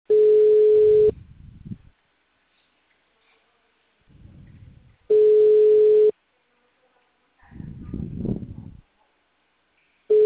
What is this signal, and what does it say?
Ringback tone (Europe, 425 Hz).
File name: out- -2005